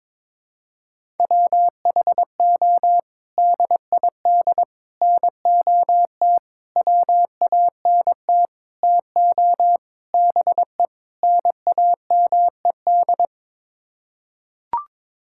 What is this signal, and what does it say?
Morse code, heard as a keyed tone, about 22 wpm, 700 Hz. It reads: WHO DID NOT WANT TO BE NAMED
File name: practice / 22wpm / Sets of 7 Words - 22wpm